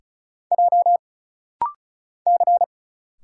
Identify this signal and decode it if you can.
Morse code, heard as a keyed tone, about 34 wpm, 700 Hz. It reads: J C
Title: Single Letters - Rapid-Fire 35wpm